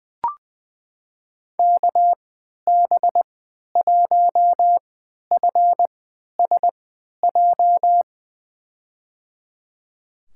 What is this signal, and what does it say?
Morse code, heard as a keyed tone, about 20 wpm, spaced as for 11 wpm, 700 Hz. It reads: KB1FSJ
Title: USA Call Signs - 6 Characters - 3x Character Spacing - 20wpm